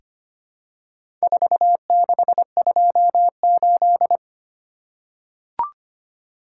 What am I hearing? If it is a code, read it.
Morse code, heard as a keyed tone, about 25 wpm, 700 Hz. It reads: 4628
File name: practice / 25wpm / Four Letter-Number Combinations - 25wpm